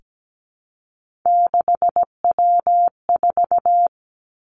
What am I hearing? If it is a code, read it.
Morse code, heard as a keyed tone, about 17 wpm, 700 Hz. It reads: 6W4
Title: Three Challenging Letter-Number Combinations 17wpm